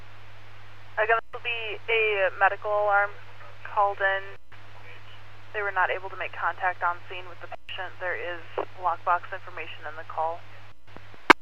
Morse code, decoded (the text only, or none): none